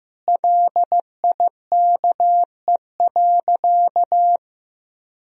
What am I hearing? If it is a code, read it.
Morse code, heard as a keyed tone, about 15 wpm, 700 Hz. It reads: LIKE.
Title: Sentences from Top 500 Words and Abbreviations 15wpm